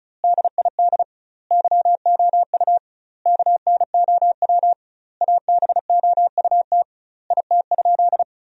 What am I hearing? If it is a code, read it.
Morse code, heard as a keyed tone, about 34 wpm, 700 Hz. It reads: DID YOU KNOW ABOUT IT?
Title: 2x Word-Spacing - Sentences from Top 100 Words - 35wpm